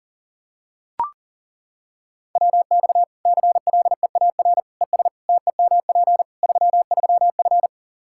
Morse code, heard as a keyed tone, about 40 wpm, 700 Hz. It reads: WX CLEAR ES TEMP 33F
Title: Practice QSO - Exchange 2 - Info - 40wpm